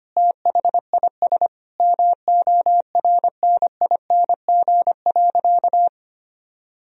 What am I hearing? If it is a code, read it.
Morse code, heard as a keyed tone, about 25 wpm, 700 Hz. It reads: THIS MORNING.
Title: Sentences from Top 1000 Words - Review Part 2 - 25wpm